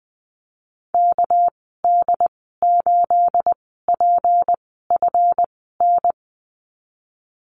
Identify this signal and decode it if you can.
Morse code, heard as a keyed tone, about 20 wpm, spaced as for 15 wpm, 700 Hz. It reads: KD8PFN
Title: 219 Lesson 35g - Number 0 - Call Signs 20wpm